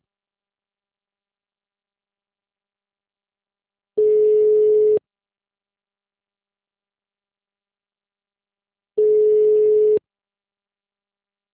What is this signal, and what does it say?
Ringback tone (Europe, 425 Hz).
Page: Jundiai - SP